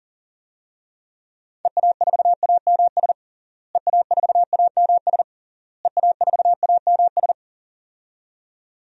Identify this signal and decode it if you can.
Morse code, heard as a keyed tone, about 40 wpm, 700 Hz. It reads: EA4AMS EA4AMS EA4AMS
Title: DXCC-100 Call Signs - 6 Characters - Repeated 3x - 40wpm